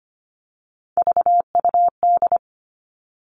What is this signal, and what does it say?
Morse code, heard as a keyed tone, about 25 wpm, 700 Hz. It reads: VUD
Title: All Cut Numbers - 3-Digit 25wpm